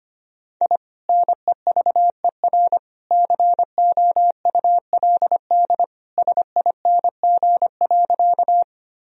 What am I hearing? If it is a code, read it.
Morse code, heard as a keyed tone, about 25 wpm, 700 Hz. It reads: I NEVER COULD SING.
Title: Sentences from Top 1000 Words - Review Part 2 - 25wpm